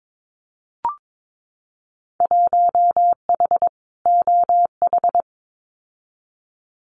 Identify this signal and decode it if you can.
Morse code, heard as a keyed tone, about 22 wpm, 700 Hz. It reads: 1H OH
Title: Winter Field Day Exchanges 22wpm